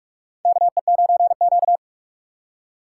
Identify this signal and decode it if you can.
Morse code, heard as a keyed tone, about 43 wpm, 700 Hz. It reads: KE9Q
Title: USA Call Signs - 4 Characters - 6x Character Spacing - 45wpm